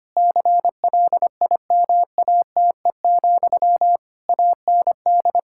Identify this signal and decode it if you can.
Morse code, heard as a keyed tone, about 25 wpm, 700 Hz. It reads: CLIMATE, AND
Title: Sentences from Top 1600 Words 25wpm